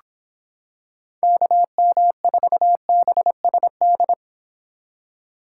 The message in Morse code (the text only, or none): KM4BSD